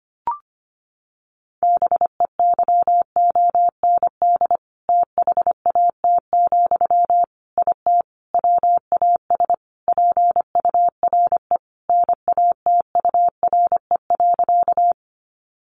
Morse code, heard as a keyed tone, about 25 wpm, 700 Hz. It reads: BEYOND THAT, IT WAS PURE NATURE.
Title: Sentences from Top 1500 Words 25wpm